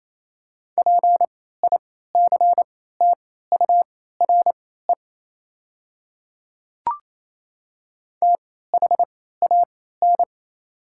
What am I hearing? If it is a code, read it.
Morse code, heard as a keyed tone, about 28 wpm, spaced as for 16 wpm, 700 Hz. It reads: PICTURE THAN
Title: 3x Character Spacing - Top 300 Words - 28wpm